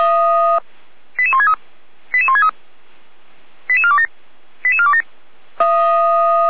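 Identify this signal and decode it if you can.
Five-tone selective call (ZVEI), heard as quick runs of five tones, tones of about 70 ms. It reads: 88163 88163 88427 88427
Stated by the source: ZVEI call received by SDR, fire brigade Feuerwehr Orlamünde (f85360).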